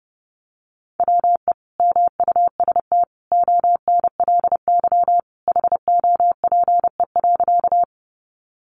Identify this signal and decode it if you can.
Morse code, heard as a keyed tone, about 30 wpm, 700 Hz. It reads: WE MUST ONLY HOPE.
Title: Sentences from Top 1000 Words 30wpm